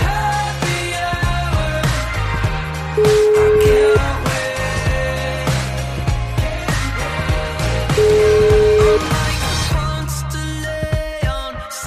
Ringback tone (Europe, 425 Hz), with noise.